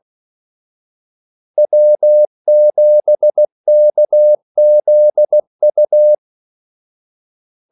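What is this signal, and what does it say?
Morse code, heard as a keyed tone, about 16 wpm, 600 Hz. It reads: W7KZU
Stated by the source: Morse call sign practice at 16 wpm.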